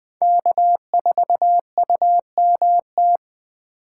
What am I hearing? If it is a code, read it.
Morse code, heard as a keyed tone, about 19 wpm, 700 Hz. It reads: K4UMT